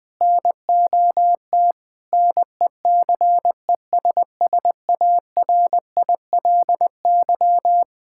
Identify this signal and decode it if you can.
Morse code, heard as a keyed tone, about 21 wpm, 700 Hz. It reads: NOT NECESSARILY
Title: Sentences with 8-Word Phrases - Spoken Beginning - 20wpm